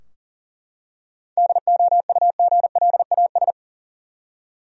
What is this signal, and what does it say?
Morse code, heard as a keyed tone, about 37 wpm, 700 Hz. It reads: DOUGLAS
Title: USA Common Names - 40wpm